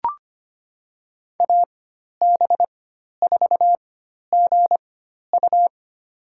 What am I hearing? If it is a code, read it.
Morse code, heard as a keyed tone, about 25 wpm, spaced as for 12 wpm, 700 Hz. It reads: AB4GU
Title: USA Call Signs - 4x Character Spacing - 25wpm